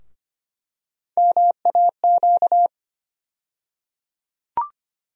Morse code, heard as a keyed tone, about 25 wpm, 700 Hz. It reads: MAQ
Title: USA Call Sign Postfixes - 3 Letters 25wpm